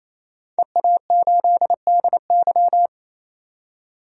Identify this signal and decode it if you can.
Morse code, heard as a keyed tone, about 28 wpm, 700 Hz. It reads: EA8DY
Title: DXCC-100 Call Signs - 5 Characters - 28wpm